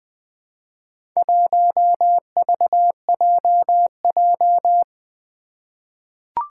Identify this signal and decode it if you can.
Morse code, heard as a keyed tone, about 20 wpm, 700 Hz. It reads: 1VJJ
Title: Four Challenging Letter-Number Combinations 20wpm